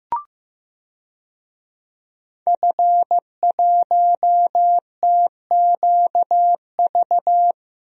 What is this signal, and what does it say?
Morse code, heard as a keyed tone, about 15 wpm, 700 Hz. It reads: F1TQV
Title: DXCC-100 Call Signs - 5 Characters - Repeated 2x - 15wpm